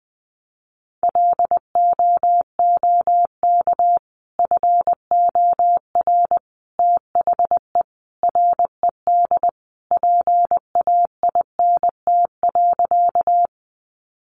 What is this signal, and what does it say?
Morse code, heard as a keyed tone, about 20 wpm, 700 Hz. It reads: LOOK FOR THE RED PAINT.
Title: A Sound Of Thunder 20wpm